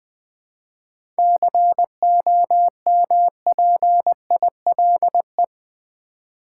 Morse code, heard as a keyed tone, about 20 wpm, 700 Hz. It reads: COMPILE
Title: 7 Letter Words - First 3 Letters Given - 20wpm